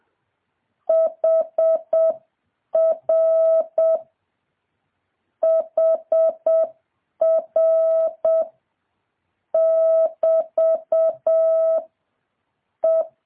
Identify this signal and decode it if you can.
Morse code, heard as a keyed tone, about 7 wpm, 650 Hz. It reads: HR HR <BT> E
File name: jh8cbh / 2023cw